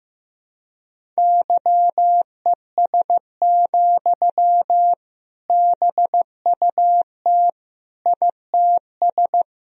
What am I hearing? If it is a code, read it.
Morse code, heard as a keyed tone, about 16 wpm, 700 Hz. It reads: YES, BUT ITS
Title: Sentences from Top 1000 Words - Review Part 1 - 15wpm